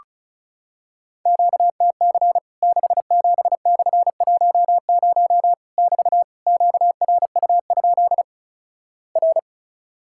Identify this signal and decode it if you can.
Morse code, heard as a keyed tone, about 36 wpm, 700 Hz. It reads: QTC 67/10 <BT> QRU? R
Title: Worked All Europe DX Contest - QTC Exchange - 35wpm